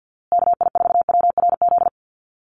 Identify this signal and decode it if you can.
Morse code, heard as a keyed tone, about 49 wpm, 700 Hz. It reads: KI4WRZ
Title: USA Call Signs Repeated 3x 50wpm